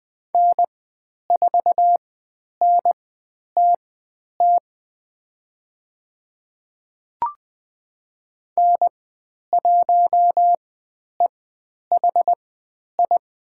Morse code, heard as a keyed tone, about 20 wpm, spaced as for 10 wpm, 700 Hz. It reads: N4NTT N1EHI
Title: 102 Lesson 17g - Letter u - Call Signs 20wpm